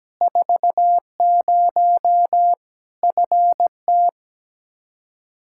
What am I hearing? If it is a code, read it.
Morse code, heard as a keyed tone, about 17 wpm, 700 Hz. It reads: 40 FT
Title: Practice QSO - Exchange 2 - Info - 17wpm